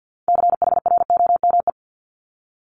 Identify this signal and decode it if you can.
Morse code, heard as a keyed tone, about 50 wpm, 690 Hz. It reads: CHROME
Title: Multi-letter Phonemes Repeated 3x 50wpm